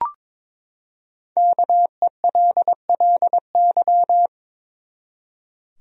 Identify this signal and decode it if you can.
Morse code, heard as a keyed tone, about 22 wpm, 700 Hz. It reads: KELLY
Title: USA Common Names - Max 5 Letters - 22wpm